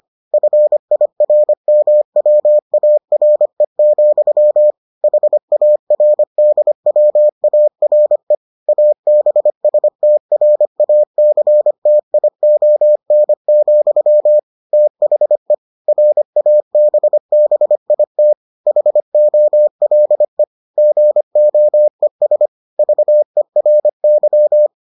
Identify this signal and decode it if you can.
Morse code, heard as a keyed tone, about 25 wpm, 600 Hz. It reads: FIRMWARE, HARDWARE ABSTRACTION, THE RABBIT HOLE GOES VERY